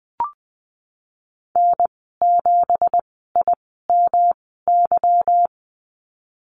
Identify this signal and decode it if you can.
Morse code, heard as a keyed tone, about 20 wpm, spaced as for 15 wpm, 700 Hz. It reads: N7IMY
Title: 173 Lesson 28g - Number 9 - Call Signs 20wpm